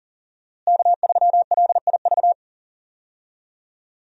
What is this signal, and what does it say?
Morse code, heard as a keyed tone, about 40 wpm, 700 Hz. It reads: K3LIV
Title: USA Call Signs - 5 Characters - Repeated 3x - 40wpm